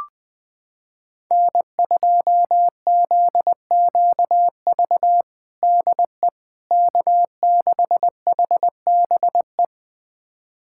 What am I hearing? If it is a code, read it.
Morse code, heard as a keyed tone, about 20 wpm, 700 Hz. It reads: N2ZQV DE K6HBE
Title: USA Callsign DE Callsign 20wpm